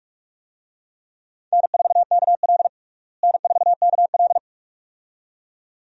Morse code, heard as a keyed tone, about 46 wpm, 700 Hz. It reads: N4KL N4KL
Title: USA Call Signs - 4 Characters - Repeated 2x - 45wpm